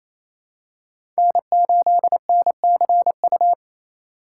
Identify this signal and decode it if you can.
Morse code, heard as a keyed tone, about 28 wpm, 700 Hz. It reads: N8NCU
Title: USA Call Signs - 5 Characters - 2x Character Spacing - 28wpm